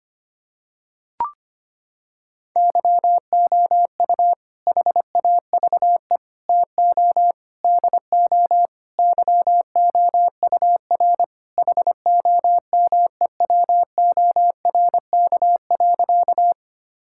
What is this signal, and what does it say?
Morse code, heard as a keyed tone, about 25 wpm, 700 Hz. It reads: YOU HAVE TO DO YOUR HOMEWORK.
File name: practice / 25wpm / Sentences with 4-Word Phrases - 25wpm